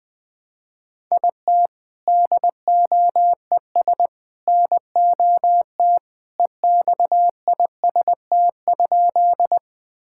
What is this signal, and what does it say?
Morse code, heard as a keyed tone, about 20 wpm, 700 Hz. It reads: IT DOES NOT EXIST?
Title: Sentences from Top 1000 Words - Review Part 5 - 20wpm